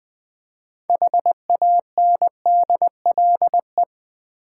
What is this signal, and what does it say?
Morse code, heard as a keyed tone, about 20 wpm, 700 Hz. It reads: HANDLE
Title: Sentences from Top 1100 Words - New Word Drill v2 - 20wpm